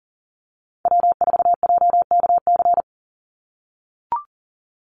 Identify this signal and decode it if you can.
Morse code, heard as a keyed tone, about 40 wpm, 700 Hz. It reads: W4JKC
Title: NA SOTA Chaser Callsigns 40wpm